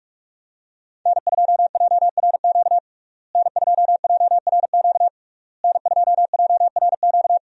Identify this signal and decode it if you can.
Morse code, heard as a keyed tone, about 46 wpm, 700 Hz. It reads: N2JRQ N2JRQ N2JRQ
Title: USA Call Signs - 5 Characters - Repeated 3x - 45wpm